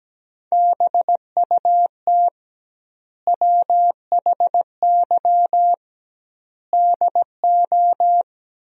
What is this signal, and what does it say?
Morse code, heard as a keyed tone, about 17 wpm, 700 Hz. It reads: BUT WHY DO
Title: 2x Word-Spacing - Sentences from Top 200 Words - 17wpm